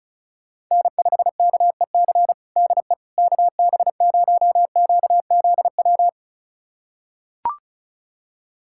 Morse code, heard as a keyed tone, about 36 wpm, 700 Hz. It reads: N5KEC DE KB0QZW
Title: USA Callsign DE Callsign 35wpm